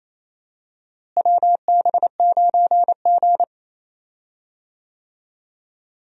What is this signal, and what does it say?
Morse code, heard as a keyed tone, about 28 wpm, 700 Hz. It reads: WB9G
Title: USA Call Signs - 4 Characters - Repeated 2x - 28wpm